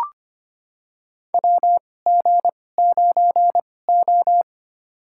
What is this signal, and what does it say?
Morse code, heard as a keyed tone, about 25 wpm, spaced as for 18 wpm, 700 Hz. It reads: WG9O
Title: USA Call Signs - 2x Character Spacing - 25wpm